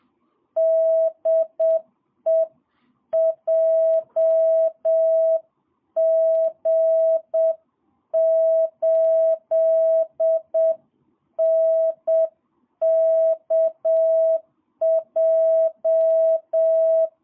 Morse code, heard as a keyed tone, about 7 wpm, 650 Hz. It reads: DE JG8NKJ
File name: jh8cbh / 2022cw